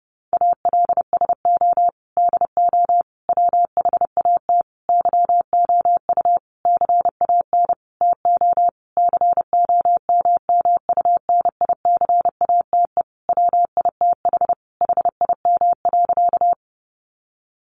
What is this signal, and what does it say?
Morse code, heard as a keyed tone, about 30 wpm, 700 Hz. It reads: ALSO DO WHAT YOU CAN TO COMMUNICATE WITH HIM.
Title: Sentences from Top 1300 Words 30wpm